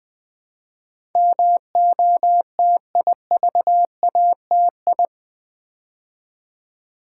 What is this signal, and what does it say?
Morse code, heard as a keyed tone, about 20 wpm, 700 Hz. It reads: MOTIVATI
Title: Build A Word - 4 Letters At A Time - 10-Letter Words - 20wpm